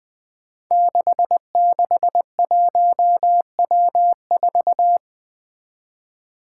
Morse code, heard as a keyed tone, about 21 wpm, 700 Hz. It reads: 661W4
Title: Five Challenging Letter-Number Combinations 20wpm